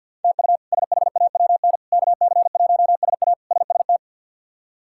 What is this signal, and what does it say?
Morse code, heard as a keyed tone, about 47 wpm, 700 Hz. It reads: TU SHAWN KC1SA SST
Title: K1USN SST QSO Practice - 50wpm